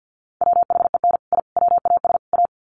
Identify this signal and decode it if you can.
Morse code, heard as a keyed tone, about 49 wpm, 700 Hz. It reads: WHEN I WAS A
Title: Sets of 4 Words 50wpm